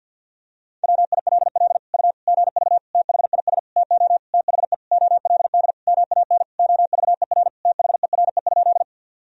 Morse code, heard as a keyed tone, about 51 wpm, 700 Hz. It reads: WILL U GV THIS TO THE OLD MAN OVER THERE?
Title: Sentences from Top 200 Words and Abbreviations 50wpm